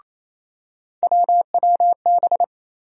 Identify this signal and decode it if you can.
Morse code, heard as a keyed tone, about 28 wpm, 700 Hz. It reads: WWB
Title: USA Call Sign Postfixes - 3 Letters 28wpm